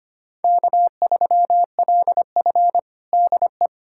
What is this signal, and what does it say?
Morse code, heard as a keyed tone, about 25 wpm, 700 Hz. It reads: K3LF DE
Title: USA Callsign DE Callsign - 4 Characters - 25wpm